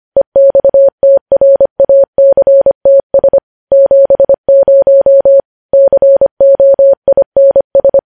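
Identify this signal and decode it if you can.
Morse code, heard as a keyed tone, about 25 wpm, 550 Hz. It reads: EXTRACTS 70 COINS
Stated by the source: Morse code practice of news headlines sent at 25 wpm.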